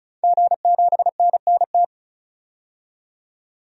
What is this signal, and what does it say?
Morse code, heard as a keyed tone, about 36 wpm, 700 Hz. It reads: G7NNT
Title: DXCC-100 Call Signs - 5 Characters - 35wpm